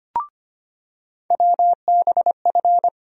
Morse code, heard as a keyed tone, about 25 wpm, 700 Hz. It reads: WBF
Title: USA Call Sign Postfixes - 3 Letters 25wpm